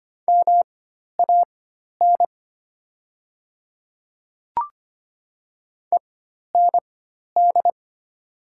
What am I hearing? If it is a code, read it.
Morse code, heard as a keyed tone, about 25 wpm, spaced as for 12 wpm, 700 Hz. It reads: MAN END
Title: 4x Character Spacing - Top 300 Words - 25wpm